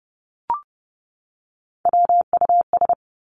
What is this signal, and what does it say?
Morse code, heard as a keyed tone, about 30 wpm, 700 Hz. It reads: WUS
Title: Three Challenging Letter-Number Combinations 30wpm